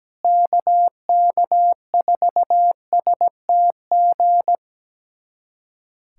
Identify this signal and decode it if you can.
Morse code, heard as a keyed tone, about 17 wpm, 700 Hz. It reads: KK4STG
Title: USA Callsign DE Callsign 17wpm